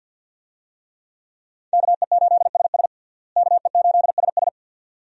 Morse code, heard as a keyed tone, about 51 wpm, 700 Hz. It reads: KE8SS KE8SS
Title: USA Call Signs - 5 Characters - Repeated 2x - 50wpm